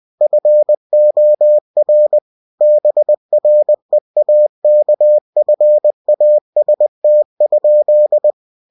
Morse code, heard as a keyed tone, about 20 wpm, 600 Hz. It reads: FOR BREAKFAST?
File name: cw-books / winnie-the-pooh / winnie-20wpm-page-0110